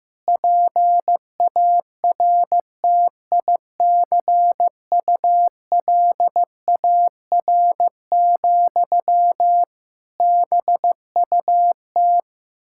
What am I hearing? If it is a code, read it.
Morse code, heard as a keyed tone, about 15 wpm, 700 Hz. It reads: PARTICULAR, BUT
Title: Sentences from Top 1000 Words - Review Part 5 - 15wpm